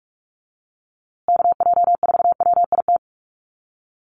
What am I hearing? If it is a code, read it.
Morse code, heard as a keyed tone, about 44 wpm, 700 Hz. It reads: KJ4WIT
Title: USA Call Signs Repeated 3x 45wpm